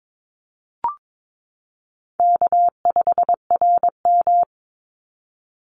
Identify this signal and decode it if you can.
Morse code, heard as a keyed tone, about 22 wpm, 700 Hz. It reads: K5RM
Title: Easy Call Signs Repeated 3x - 22wpm